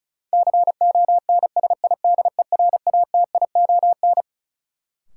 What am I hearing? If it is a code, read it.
Morse code, heard as a keyed tone, about 36 wpm, 700 Hz. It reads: CONSIDERATION
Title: Sentences from Top 1500 Words - New Word Drill Alt - 35wpm